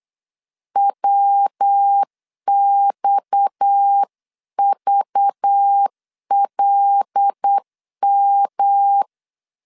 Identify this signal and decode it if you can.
Morse code, heard as a keyed tone, about 8 wpm, 800 Hz. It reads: WXVLM
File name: LEZIONI / 6 PKX0 / 6 PKX 8wpm 3041